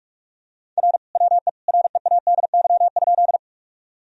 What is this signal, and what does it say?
Morse code, heard as a keyed tone, about 46 wpm, 700 Hz. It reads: R WE READY?
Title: Sentences from Top 400 Words and Abbreviations 45wpm